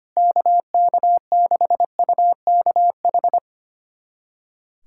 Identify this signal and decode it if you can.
Morse code, heard as a keyed tone, about 25 wpm, 700 Hz. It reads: KK6UKH
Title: USA Callsign DE Callsign - 6 Characters - 25wpm